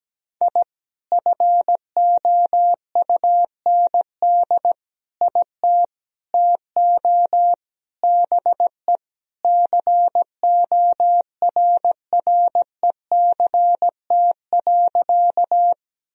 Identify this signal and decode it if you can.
Morse code, heard as a keyed tone, about 17 wpm, 700 Hz. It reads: I FOUND IT TO BE CORRECT.
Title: Sentences from Top 1000 Words - Review Part 2 - 17wpm